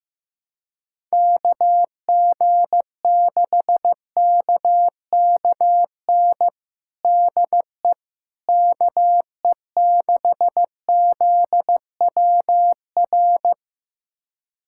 Morse code, heard as a keyed tone, about 15 wpm, 700 Hz. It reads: KG6KKN DE KE6ZWR
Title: USA Callsign DE Callsign - 6 Characters - 15wpm